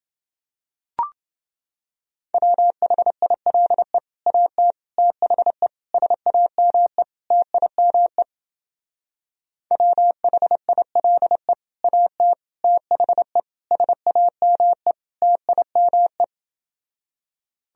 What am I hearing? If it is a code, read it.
Morse code, heard as a keyed tone, about 29 wpm, 700 Hz. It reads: WHILE AT THE SAME TIME WHILE AT THE SAME TIME